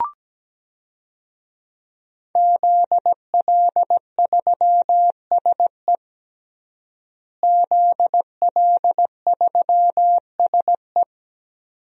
Morse code, heard as a keyed tone, about 17 wpm, 700 Hz. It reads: ZL3SE ZL3SE
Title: DXCC-100 Call Signs - 5 Characters - Repeated 2x - 17wpm